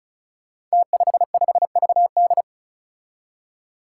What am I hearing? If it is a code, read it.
Morse code, heard as a keyed tone, about 36 wpm, 700 Hz. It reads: T55VD
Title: Five Challenging Letter-Number Combinations 35wpm